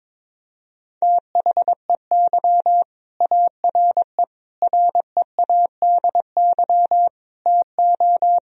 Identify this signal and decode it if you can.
Morse code, heard as a keyed tone, about 22 wpm, 700 Hz. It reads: THEY ARE READY TO
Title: Sentences from Top 1000 Words - Review Part 3 - 22wpm